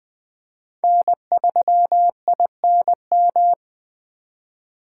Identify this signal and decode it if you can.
Morse code, heard as a keyed tone, about 20 wpm, 700 Hz. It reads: N3INM